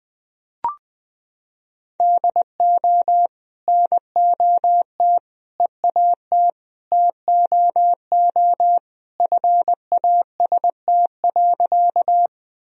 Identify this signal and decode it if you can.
Morse code, heard as a keyed tone, about 21 wpm, 700 Hz. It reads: DO NOT EAT TOO FAST.
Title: Sentences from Top 1000 Words - Review Part 2 - 20wpm